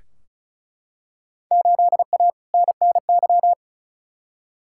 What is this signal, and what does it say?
Morse code, heard as a keyed tone, about 36 wpm, 700 Hz. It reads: 8A NNY
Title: Field Day Exchanges Encore - 35wpm